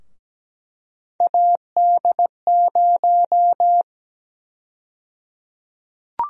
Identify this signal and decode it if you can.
Morse code, heard as a keyed tone, about 17 wpm, 700 Hz. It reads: AD0
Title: USA Call Sign Prefixes - 17wpm